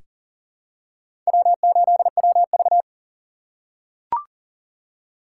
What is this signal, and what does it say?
Morse code, heard as a keyed tone, about 40 wpm, 700 Hz. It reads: W8WV
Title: USA Call Signs - 4 Characters - 6x Character Spacing - 40wpm